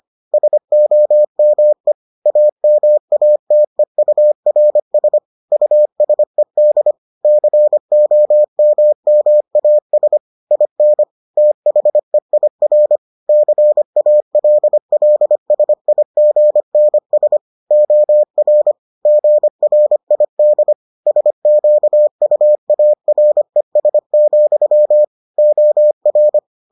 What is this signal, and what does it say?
Morse code, heard as a keyed tone, about 25 wpm, 600 Hz. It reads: SOME AMATEURS USED COMMAS IN THEIR CALLSIGNS OR GRID SQUARES, OR